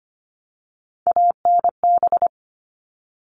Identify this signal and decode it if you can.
Morse code, heard as a keyed tone, about 25 wpm, 700 Hz. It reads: ANB
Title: All Cut Numbers - 3-Digit 25wpm